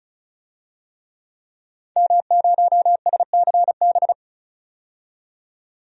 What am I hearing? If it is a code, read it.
Morse code, heard as a keyed tone, about 34 wpm, 700 Hz. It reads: M0SCB
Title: DXCC-100 Call Signs - 5 Characters - 35wpm